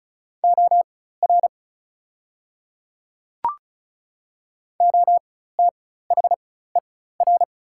Morse code, heard as a keyed tone, about 36 wpm, spaced as for 16 wpm, 700 Hz. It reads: OR OTHER
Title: 4x Character Spacing - Top 100 Words - 35wpm